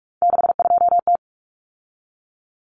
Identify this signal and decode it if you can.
Morse code, heard as a keyed tone, about 44 wpm, 700 Hz. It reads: 62T
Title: Common Cut Numbers - 3-Digit 45wpm